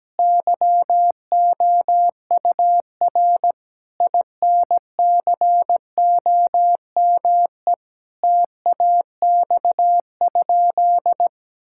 Morse code, heard as a keyed tone, about 17 wpm, 700 Hz. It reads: YOUR INCOME TAX?
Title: Sentences from Top 1000 Words - Review Part 5 - 17wpm